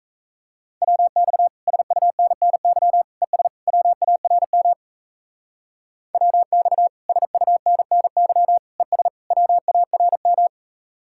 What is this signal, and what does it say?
Morse code, heard as a keyed tone, about 37 wpm, 700 Hz. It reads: WX SUNNY ES WARM WX SUNNY ES WARM